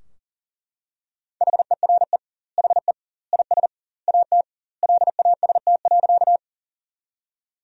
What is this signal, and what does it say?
Morse code, heard as a keyed tone, about 40 wpm, 700 Hz. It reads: HERE HE IS AT LAST.
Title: 2x Word-Spacing - Sentences from Top 300 Words - 40wpm